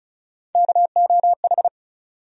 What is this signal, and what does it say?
Morse code, heard as a keyed tone, about 36 wpm, 700 Hz. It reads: KOH
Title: USA Call Sign Postfixes - 3 Letters 35wpm